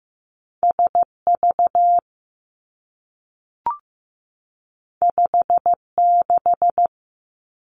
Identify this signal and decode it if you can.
Morse code, heard as a keyed tone, about 15 wpm, 700 Hz. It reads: S V 5 6
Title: Two Challenging Letter-Number Combinations 15wpm